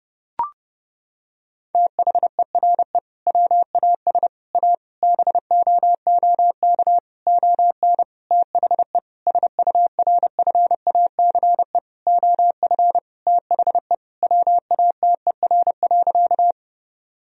Morse code, heard as a keyed tone, about 32 wpm, 700 Hz. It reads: THERE WAS A BOOK ON THE SURFACE OF THE WATER.
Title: Sentences from Top 1000 Words - Review Part 3 - 30wpm